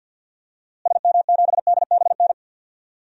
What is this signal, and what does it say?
Morse code, heard as a keyed tone, about 46 wpm, 700 Hz. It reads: SM7DBN
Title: DXCC-100 Call Signs - 6 Characters - Repeated 3x - 50wpm